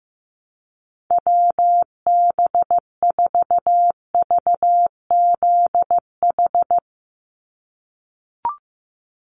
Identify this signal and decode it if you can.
Morse code, heard as a keyed tone, about 15 wpm, 700 Hz. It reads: WB4VZH
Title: USA Call Signs 15wpm